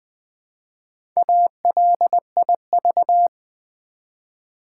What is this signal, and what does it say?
Morse code, heard as a keyed tone, about 20 wpm, 700 Hz. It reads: ALIV
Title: Build A Word - 1 Letter At A Time - 3 to 5-Letter Words - 20wpm